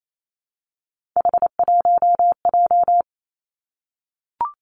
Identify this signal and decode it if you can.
Morse code, heard as a keyed tone, about 28 wpm, 700 Hz. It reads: H1J
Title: Three Challenging Letter-Number Combinations 28wpm